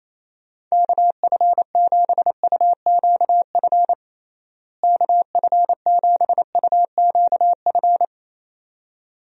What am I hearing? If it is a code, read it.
Morse code, heard as a keyed tone, about 28 wpm, 700 Hz. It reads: KF7UQF KF7UQF
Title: USA Call Signs - 6 Characters - Repeated 2x - 28wpm